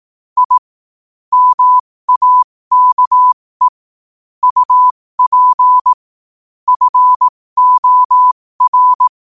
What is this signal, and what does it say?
Morse code, heard as a keyed tone, about 18 wpm, spaced as for 15 wpm, 1000 Hz. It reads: I MAKE UP FOR
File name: qotdfiles / 15wpm